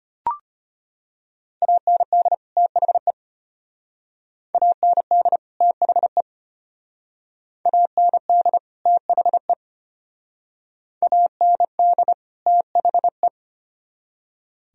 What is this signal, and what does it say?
Morse code, heard as a keyed tone, about 34 wpm, 700 Hz. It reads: AND THE AND THE AND THE AND THE